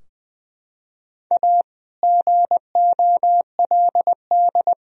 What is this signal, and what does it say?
Morse code, heard as a keyed tone, about 20 wpm, 700 Hz. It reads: A GOLD
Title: Sentences from Top 1000 Words - Review Part 3 - 20wpm